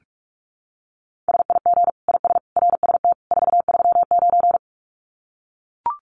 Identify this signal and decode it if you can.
Morse code, heard as a keyed tone, about 44 wpm, 700 Hz. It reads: SIG IS RST 439
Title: Common QSO Phrases 45wpm